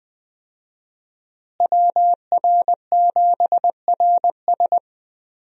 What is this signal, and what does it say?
Morse code, heard as a keyed tone, about 20 wpm, 700 Hz. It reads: WR7RS
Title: USA Call Signs - 5 Characters - Repeated 3x - 20wpm